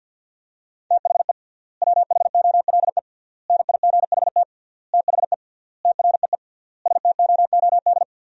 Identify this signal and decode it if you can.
Morse code, heard as a keyed tone, about 53 wpm, 700 Hz. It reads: THE WHOLE NIGHT THE TREE STOOD
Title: 3x Word-Spacing - Sentences from Top 500 Words - 50wpm